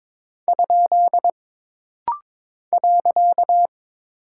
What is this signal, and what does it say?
Morse code, heard as a keyed tone, about 22 wpm, 700 Hz. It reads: ? .
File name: practice / 22wpm / Single Punctuation-Prosign - Rapid-Fire - 22wpm